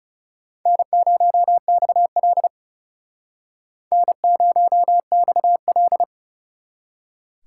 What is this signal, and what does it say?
Morse code, heard as a keyed tone, about 37 wpm, 700 Hz. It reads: N0XL N0XL
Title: Speed Racing - USA Call Signs 30wpm